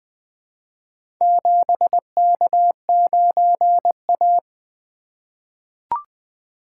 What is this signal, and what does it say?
Morse code, heard as a keyed tone, about 20 wpm, 700 Hz. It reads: 7K9A
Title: Four Letter-Number Combinations - 20wpm